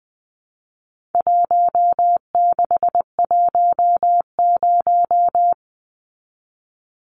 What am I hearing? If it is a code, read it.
Morse code, heard as a keyed tone, about 20 wpm, 700 Hz. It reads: 1610
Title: Numbers - 4-Digit 20wpm